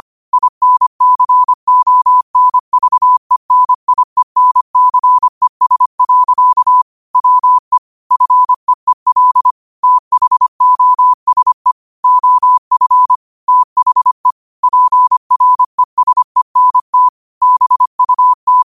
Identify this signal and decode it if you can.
Morse code, heard as a keyed tone, about 25 wpm, 1000 Hz. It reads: INCONVENIENCES. WE FEEL THOSE OF THE PRESENT BUT
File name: qotdfiles / 25wpm